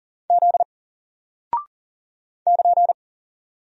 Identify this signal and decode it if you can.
Morse code, heard as a keyed tone, about 40 wpm, 700 Hz. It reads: Z <KN>
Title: Single Character (Letter:Number:Punctuation:Prosign) Alt 40wpm